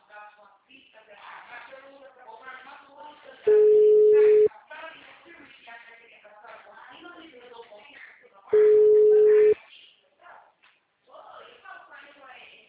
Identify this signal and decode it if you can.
Ringback tone (Europe, 425 Hz), with speech.
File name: out- -2013- - - .69239